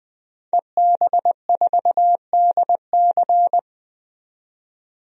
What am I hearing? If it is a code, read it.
Morse code, heard as a keyed tone, about 20 wpm, 700 Hz. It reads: EB4DC
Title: DXCC-100 Call Signs - 5 Characters - 20wpm